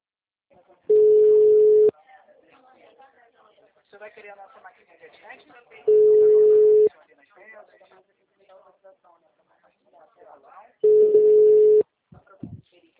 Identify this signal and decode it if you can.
Ringback tone (Europe, 425 Hz), with speech.